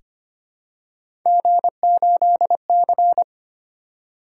Morse code, heard as a keyed tone, about 25 wpm, 700 Hz. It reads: G8C
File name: Three Letter-Number Combinations - 25wpm